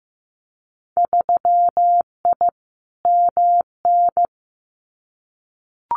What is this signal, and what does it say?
Morse code, heard as a keyed tone, about 15 wpm, 700 Hz. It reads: 3I MN
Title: Winter Field Day Exchanges 15wpm